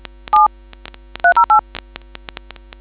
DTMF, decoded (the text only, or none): 7308